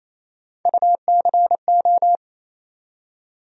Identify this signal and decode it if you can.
Morse code, heard as a keyed tone, about 28 wpm, 700 Hz. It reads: UCO
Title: USA Call Sign Postfixes - 3 Letters 28wpm